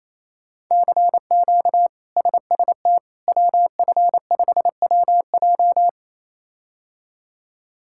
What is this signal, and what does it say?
Morse code, heard as a keyed tone, about 28 wpm, 700 Hz. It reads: CQ SST WF5WJ
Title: K1USN SST QSO Practice - 28wpm